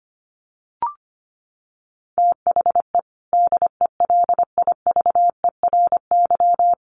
Morse code, heard as a keyed tone, about 25 wpm, 700 Hz. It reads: THE DELIVERY
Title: Sentences from Top 1500 Words 25wpm